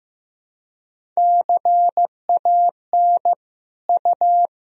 Morse code, heard as a keyed tone, about 16 wpm, 700 Hz. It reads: CAN U
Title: Sentences from Top 400 Words and Abbreviations 15wpm